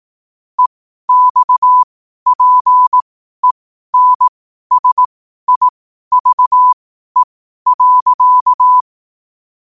Morse code, heard as a keyed tone, about 18 wpm, spaced as for 12 wpm, 1000 Hz. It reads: EXPENSIVE.
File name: qotdfiles / 12wpm